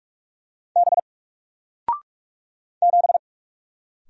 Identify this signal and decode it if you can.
Morse code, heard as a keyed tone, about 45 wpm, 700 Hz. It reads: B 7
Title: Single Letter-Number 45wpm